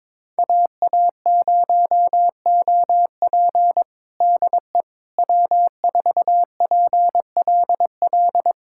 Morse code, heard as a keyed tone, about 22 wpm, 700 Hz. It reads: AA0OP DE W4PLL
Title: USA Callsign DE Callsign - 5 Characters - 22wpm